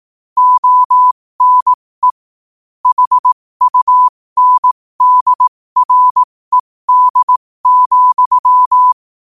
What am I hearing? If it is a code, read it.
Morse code, heard as a keyed tone, about 18 wpm, spaced as for 15 wpm, 1000 Hz. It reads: ONE HUNDRED,
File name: qotdfiles / 15wpm